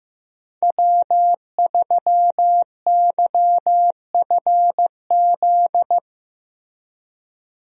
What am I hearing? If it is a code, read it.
Morse code, heard as a keyed tone, about 16 wpm, 700 Hz. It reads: W3YFZ
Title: USA Call Signs - 5 Characters - Repeated 3x - 15wpm